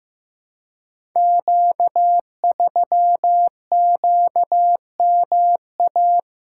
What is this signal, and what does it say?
Morse code, heard as a keyed tone, about 15 wpm, 700 Hz. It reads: Q3QMA
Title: Code Groups - 5 Letter:Numbers - 15wpm